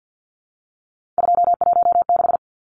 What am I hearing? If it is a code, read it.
Morse code, heard as a keyed tone, about 50 wpm, 700 Hz. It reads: 216
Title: Numbers - 3-Digit 50wpm